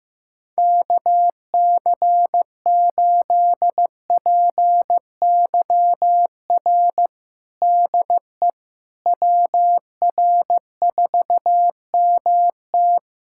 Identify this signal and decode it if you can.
Morse code, heard as a keyed tone, about 15 wpm, 700 Hz. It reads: KC8PYR DE WR4MT
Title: USA Callsign DE Callsign 15wpm